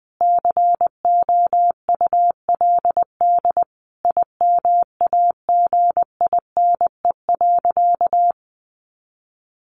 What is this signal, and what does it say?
Morse code, heard as a keyed tone, about 20 wpm, 700 Hz. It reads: COULD IMAGINE.
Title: Sentences from Top 1400 Words 20wpm